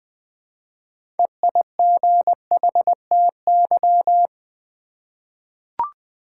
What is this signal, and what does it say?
Morse code, heard as a keyed tone, about 20 wpm, 700 Hz. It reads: EIGHTY
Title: 6 Letter Words - First 3 Letters Given - 20wpm